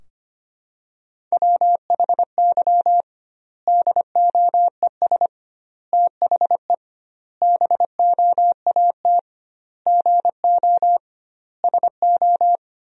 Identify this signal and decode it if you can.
Morse code, heard as a keyed tone, about 25 wpm, 700 Hz. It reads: WHY DOES THE BOAT GO SO
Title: 2x Word-Spacing - Sentences from Top 500 Words - 25wpm